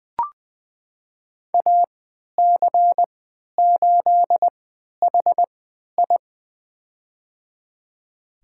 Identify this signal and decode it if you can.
Morse code, heard as a keyed tone, about 20 wpm, spaced as for 11 wpm, 700 Hz. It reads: AC8HI
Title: USA Call Signs - 5 Characters - 3x Character Spacing - 20wpm